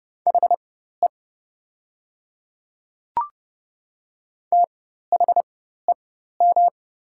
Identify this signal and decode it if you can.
Morse code, heard as a keyed tone, about 30 wpm, spaced as for 14 wpm, 700 Hz. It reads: HE THEM
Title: 4x Character Spacing - Top 100 Words - 30wpm